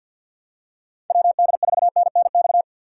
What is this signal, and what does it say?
Morse code, heard as a keyed tone, about 47 wpm, 700 Hz. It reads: WD4NNX
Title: USA Call Signs - 4x Character Spacing - 50wpm